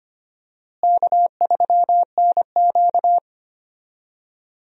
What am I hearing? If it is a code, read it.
Morse code, heard as a keyed tone, about 25 wpm, 700 Hz. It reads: K3NQ
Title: USA Call Signs - 4 Characters - Repeated 2x - 25wpm